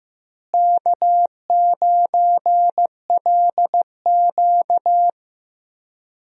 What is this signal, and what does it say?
Morse code, heard as a keyed tone, about 15 wpm, 700 Hz. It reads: K9LQ
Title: USA Call Signs - 4 Characters - Repeated 3x - 15wpm